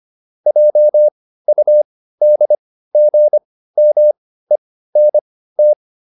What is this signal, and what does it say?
Morse code, heard as a keyed tone, about 25 wpm, spaced as for 15 wpm, 600 Hz. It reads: JUDGMENT